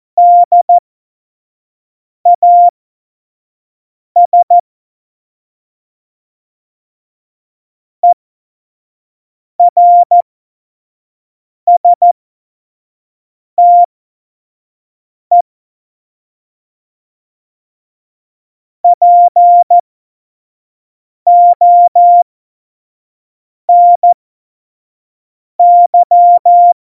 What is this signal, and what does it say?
Morse code, heard as a keyed tone, about 14 wpm, spaced as for 5 wpm, 700 Hz. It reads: DAS ERSTE PONY
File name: Morsekurs / Witze / Witz 07 Juli 2020, 25 F70